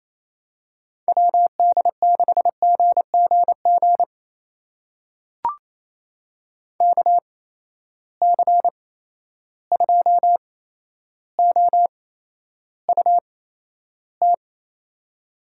Morse code, heard as a keyed tone, about 28 wpm, spaced as for 12 wpm, 700 Hz. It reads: WD6GGG KC2OUT
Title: USA Call Signs - 6 Characters - 8x Character Spacing - 28wpm